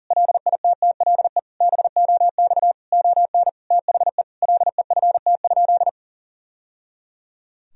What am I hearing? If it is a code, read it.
Morse code, heard as a keyed tone, about 40 wpm, 700 Hz. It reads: LITTLE BOX ON THE LEFT?
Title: Sentences from Top 1000 Words - Review Part 2 - 40wpm